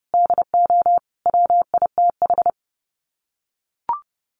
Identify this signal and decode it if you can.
Morse code, heard as a keyed tone, about 28 wpm, 700 Hz. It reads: DO WITH
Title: Sets of 5 Words Encore 30wpm